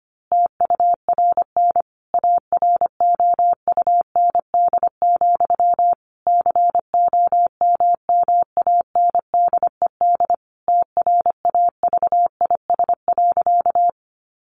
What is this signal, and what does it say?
Morse code, heard as a keyed tone, about 25 wpm, 700 Hz. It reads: TURN AROUND, COMMANDED TRAVIS.
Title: A Sound Of Thunder 25wpm